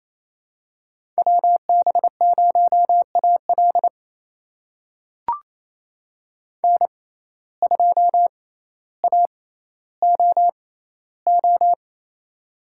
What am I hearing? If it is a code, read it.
Morse code, heard as a keyed tone, about 28 wpm, spaced as for 14 wpm, 700 Hz. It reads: WB0AL N2AOO
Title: USA Call Signs - 5 Characters - 6x Character Spacing - 28wpm